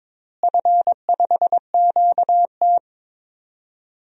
Morse code, heard as a keyed tone, about 22 wpm, 700 Hz. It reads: F5QT
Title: DXCC-100 Call Signs - 4 Characters - Repeated 3x - 22wpm